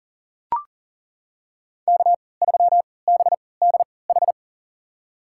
Morse code, heard as a keyed tone, about 40 wpm, spaced as for 23 wpm, 700 Hz. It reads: K3BDH
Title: USA Call Signs - 5 Characters - 3x Character Spacing - 40wpm